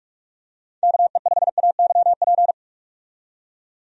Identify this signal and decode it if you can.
Morse code, heard as a keyed tone, about 41 wpm, 700 Hz. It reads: KE5AYP
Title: USA Call Signs - 6 Characters - 45wpm